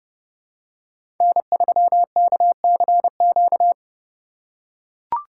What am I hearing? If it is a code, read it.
Morse code, heard as a keyed tone, about 30 wpm, 700 Hz. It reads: N3KCQ
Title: USA Call Signs - 5 Characters - 4x Character Spacing - 30wpm